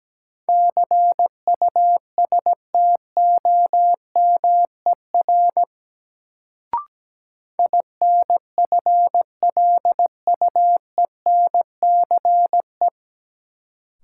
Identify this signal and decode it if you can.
Morse code, heard as a keyed tone, about 17 wpm, 700 Hz. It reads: CUSTOMER INFLUENCE
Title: Sentences from Top 1100 Words - New Word Drill - 17wpm